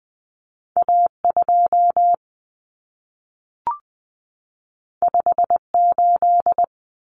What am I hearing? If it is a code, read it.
Morse code, heard as a keyed tone, about 20 wpm, 700 Hz. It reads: A2 58
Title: Common Cut Numbers - 2-Digit 20wpm